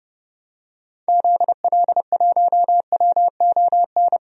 Morse code, heard as a keyed tone, about 30 wpm, 700 Hz. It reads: ZL1WON
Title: DXCC-100 Call Signs - 6 Characters - 30wpm